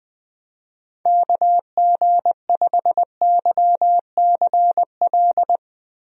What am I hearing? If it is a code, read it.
Morse code, heard as a keyed tone, about 20 wpm, 700 Hz. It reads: KG5YCL
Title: USA Call Signs - 6 Characters - Repeated 3x - 20wpm